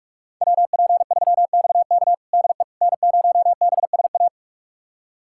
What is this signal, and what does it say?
Morse code, heard as a keyed tone, about 46 wpm, 700 Hz. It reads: WP3XK DE N0BSA
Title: USA Callsign DE Callsign - 5 Characters - 45wpm